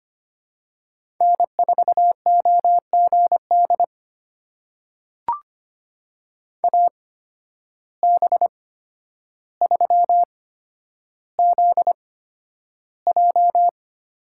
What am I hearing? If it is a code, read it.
Morse code, heard as a keyed tone, about 25 wpm, spaced as for 11 wpm, 700 Hz. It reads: N4OGD AB3ZJ
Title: USA Call Signs - 8x Character Spacing - 25wpm